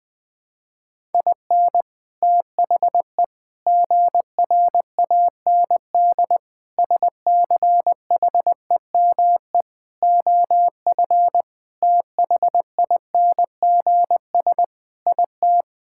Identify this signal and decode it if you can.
Morse code, heard as a keyed tone, about 21 wpm, 700 Hz. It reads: IN THE GRAND SCHEME OF THINGS IT
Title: Sentences with 6-Word Phrases - 20wpm